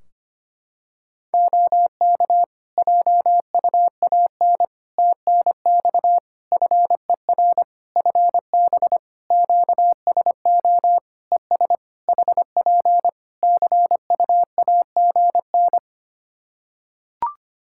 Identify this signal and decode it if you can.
Morse code, heard as a keyed tone, about 25 wpm, 700 Hz. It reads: OK JUAN TNX FER FB QSO ES HP CU AGN
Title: Practice QSO - Exchange 4 - Ending - 25wpm